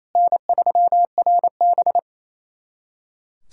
Morse code, heard as a keyed tone, about 28 wpm, 690 Hz. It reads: N3RB
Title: SOTA QSO Practice 28wpm